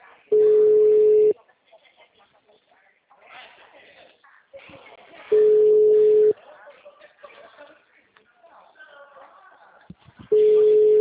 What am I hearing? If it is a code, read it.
Ringback tone (Europe, 425 Hz), with speech.
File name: out- -2008- - - .9842